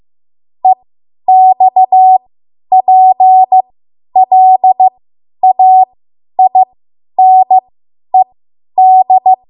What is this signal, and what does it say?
Morse code, heard as a keyed tone, about 15 wpm, spaced as for 10 wpm, 750 Hz. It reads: EXPLAINED
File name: Archive / 10 WPM / 10WPM